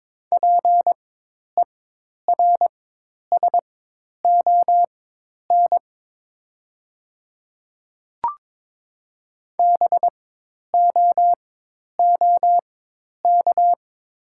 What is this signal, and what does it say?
Morse code, heard as a keyed tone, about 22 wpm, spaced as for 10 wpm, 700 Hz. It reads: PERSON BOOK
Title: 4x Character Spacing - Top 500 Words - 22wpm